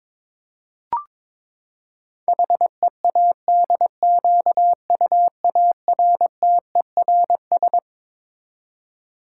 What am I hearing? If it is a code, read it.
Morse code, heard as a keyed tone, about 22 wpm, 700 Hz. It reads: HEADQUARTERS
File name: Twelve Letter Words - 22wpm